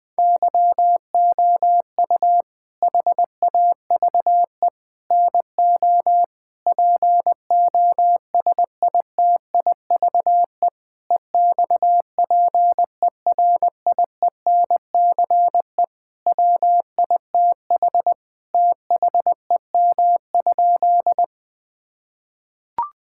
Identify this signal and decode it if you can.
Morse code, heard as a keyed tone, about 21 wpm, 700 Hz. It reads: YOU HAVE NO POSITIVE EXPERIENCE WITH THEM?
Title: Sentences from Top 1000 Words - Review Part 5 - 20wpm